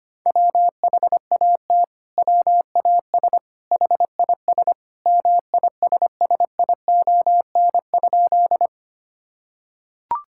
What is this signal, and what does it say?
Morse code, heard as a keyed tone, about 25 wpm, 700 Hz. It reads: WHAT WAS HIS MISSION?
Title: Sentences from Top 1600 Words 25wpm